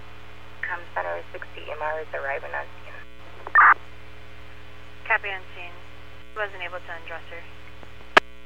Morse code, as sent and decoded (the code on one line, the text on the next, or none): none